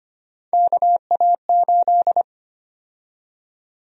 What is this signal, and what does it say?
Morse code, heard as a keyed tone, about 25 wpm, 700 Hz. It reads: KA8
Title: USA Call Sign Prefixes - 25wpm